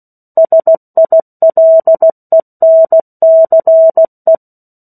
Morse code, heard as a keyed tone, about 16 wpm, 650 Hz. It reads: SILENCE
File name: 03-Chapter-03 16 WPM